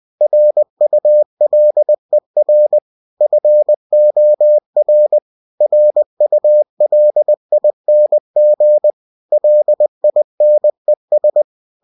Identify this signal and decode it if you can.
Morse code, heard as a keyed tone, about 20 wpm, 600 Hz. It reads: RULER FOR RULING LINES
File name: cw-books / winnie-the-pooh / winnie-20wpm-page-0109